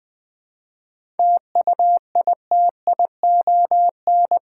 Morse code, heard as a keyed tone, about 20 wpm, 700 Hz. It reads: TUITION
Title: 7 Letter Words - First 3 Letters Given - 20wpm